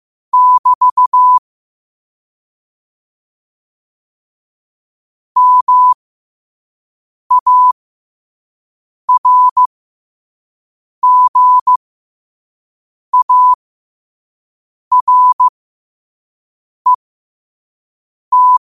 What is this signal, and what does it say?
Morse code, heard as a keyed tone, about 15 wpm, spaced as for 5 wpm, 1000 Hz. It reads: <BT> MARGARET
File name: qotdfiles / 5wpm